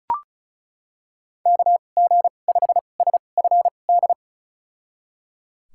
Morse code, heard as a keyed tone, about 36 wpm, spaced as for 25 wpm, 700 Hz. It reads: KG5SFD
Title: USA Call Signs - 6 Characters - 2x Character Spacing - 35wpm